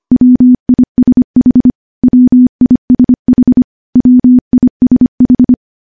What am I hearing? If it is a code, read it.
Morse code, heard as a keyed tone, about 25 wpm, 250 Hz. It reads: WISH WISH WISH
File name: morse-training-25WPM